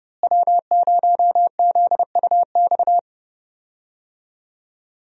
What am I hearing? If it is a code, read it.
Morse code, heard as a keyed tone, about 31 wpm, 700 Hz. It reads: W0ZUX
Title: USA Call Signs - 5 Characters - Repeated 2x - 30wpm